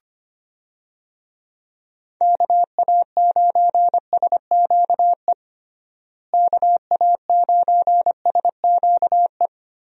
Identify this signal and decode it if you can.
Morse code, heard as a keyed tone, about 25 wpm, 700 Hz. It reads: KA9SQE KA9SQE
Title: USA Call Signs - 6 Characters - Repeated 2x - 25wpm